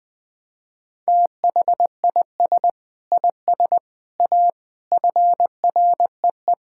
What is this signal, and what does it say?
Morse code, heard as a keyed tone, about 20 wpm, 700 Hz. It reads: THIS IS A FREE
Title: Sentences from Top 1000 Words - Review Part 6 - 20wpm